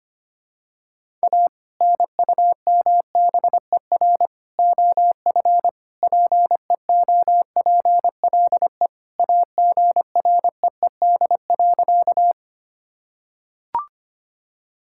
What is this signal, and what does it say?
Morse code, heard as a keyed tone, about 25 wpm, 700 Hz. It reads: A NUMBER OF PEOPLE AGREED.
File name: practice / 25wpm / Sentences with 3-Word Phrases - 25wpm